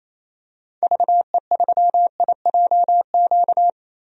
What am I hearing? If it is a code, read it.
Morse code, heard as a keyed tone, about 28 wpm, 700 Hz. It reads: VE3IJQ
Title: DXCC-100 Call Signs - 6 Characters - Repeated 2x - 28wpm